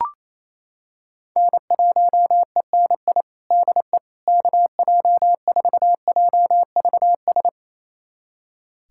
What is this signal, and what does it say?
Morse code, heard as a keyed tone, about 28 wpm, 700 Hz. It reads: N1ENI DE KJ4JVS
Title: USA Callsign DE Callsign 28wpm